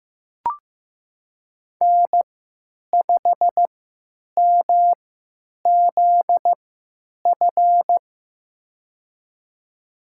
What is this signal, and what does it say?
Morse code, heard as a keyed tone, about 15 wpm, spaced as for 8 wpm, 700 Hz. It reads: N5MZF
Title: USA Call Signs - 5 Characters - 3x Character Spacing - 15wpm